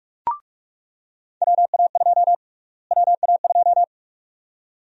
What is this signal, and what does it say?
Morse code, heard as a keyed tone, about 45 wpm, 690 Hz. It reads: WA2 WA2
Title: USA Call Sign Prefixes - Repeated 2x - 45wpm